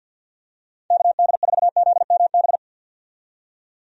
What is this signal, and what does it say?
Morse code, heard as a keyed tone, about 46 wpm, 700 Hz. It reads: KD4ZMB
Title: USA Call Signs - 6 Characters - 50wpm